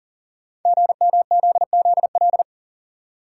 Morse code, heard as a keyed tone, about 39 wpm, 690 Hz. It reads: GMZZL
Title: Code Groups - 5 Letters - 40wpm